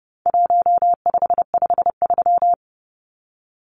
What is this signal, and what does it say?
Morse code, heard as a keyed tone, about 29 wpm, 700 Hz. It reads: 1553
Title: Numbers - 4-Digit 30wpm